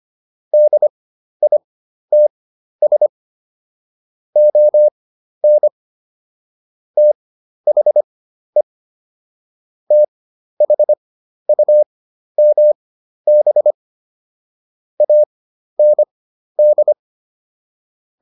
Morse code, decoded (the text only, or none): DITS ON THE THUMB AND